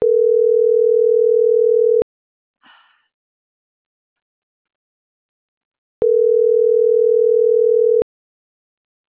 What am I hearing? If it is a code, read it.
Ringback tone (North America), 440 + 480 Hz.